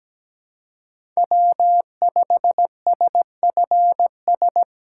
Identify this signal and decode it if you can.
Morse code, heard as a keyed tone, about 17 wpm, 700 Hz. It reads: W5SFS
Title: USA Call Signs - 5 Characters - 8x Character Spacing - 17wpm